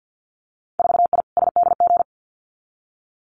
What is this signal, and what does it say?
Morse code, heard as a keyed tone, about 50 wpm, 700 Hz. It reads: 4I SDG
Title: Winter Field Day Exchanges Encore 50wpm